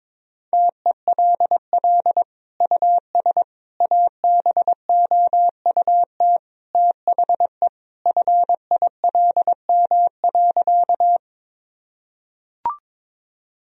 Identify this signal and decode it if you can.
Morse code, heard as a keyed tone, about 22 wpm, 700 Hz. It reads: TELL US ABOUT THE FILM.
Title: Sentences from Top 1000 Words - Review Part 3 - 22wpm